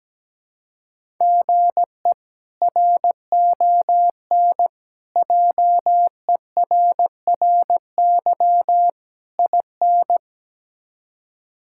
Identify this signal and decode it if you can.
Morse code, heard as a keyed tone, about 17 wpm, 700 Hz. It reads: GE RON JERRY IN
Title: K1USN SST QSO Practice - 17wpm